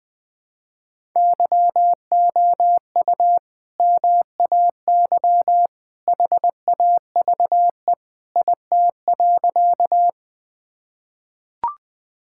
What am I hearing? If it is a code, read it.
Morse code, heard as a keyed tone, about 21 wpm, 700 Hz. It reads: YOU MAY HAVE IT.
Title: Sentences with 2-Word Phrases - 20wpm